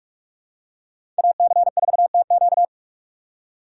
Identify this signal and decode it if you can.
Morse code, heard as a keyed tone, about 43 wpm, 700 Hz. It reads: AC4TQ
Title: USA Call Signs - 5 Characters - 6x Character Spacing - 45wpm